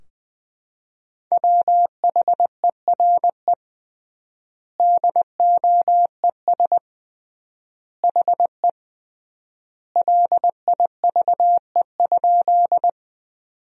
Morse code, heard as a keyed tone, about 21 wpm, 700 Hz. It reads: WHERE DOES HE LIVE?
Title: 3x Word-Spacing - Sentences from Top 200 Words - 20wpm